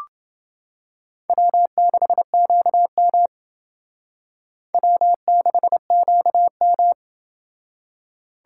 Morse code, heard as a keyed tone, about 29 wpm, 700 Hz. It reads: W6QM W6QM